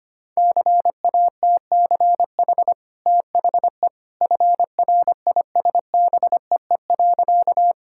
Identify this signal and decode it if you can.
Morse code, heard as a keyed tone, about 25 wpm, 690 Hz. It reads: CATCH THE FRISBEE.
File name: Easy Sentences with Hints - 25wpm